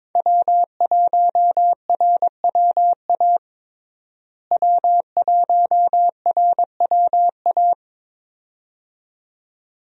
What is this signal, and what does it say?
Morse code, heard as a keyed tone, about 21 wpm, 700 Hz. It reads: W1RWA W1RWA
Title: USA Call Signs - 5 Characters - Repeated 2x - 22wpm